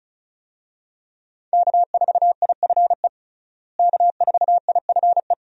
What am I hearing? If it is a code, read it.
Morse code, heard as a keyed tone, about 36 wpm, 700 Hz. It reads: K4IFE K4IFE
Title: USA Call Signs - 5 Characters - Repeated 2x - 35wpm